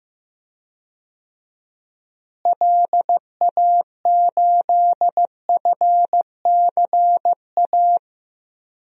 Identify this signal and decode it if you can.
Morse code, heard as a keyed tone, about 16 wpm, 700 Hz. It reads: LA8FCA
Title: DXCC-100 Call Signs - 6 Characters - Repeated 2x - 15wpm